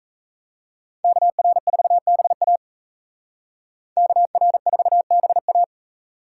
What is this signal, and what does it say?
Morse code, heard as a keyed tone, about 36 wpm, 700 Hz. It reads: KR4BA KR4BA